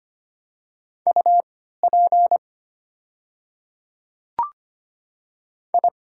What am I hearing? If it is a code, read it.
Morse code, heard as a keyed tone, about 25 wpm, spaced as for 14 wpm, 690 Hz. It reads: UP I
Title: 3x Character Spacing - Top 100 Words - 25wpm